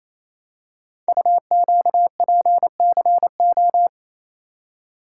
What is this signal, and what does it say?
Morse code, heard as a keyed tone, about 27 wpm, 700 Hz. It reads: UQPCO
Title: Code Groups - 5 Letters - 28wpm